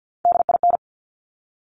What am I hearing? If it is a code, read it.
Morse code, heard as a keyed tone, about 50 wpm, 700 Hz. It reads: DIN
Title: Top Three Letter Combinations in Words 50wpm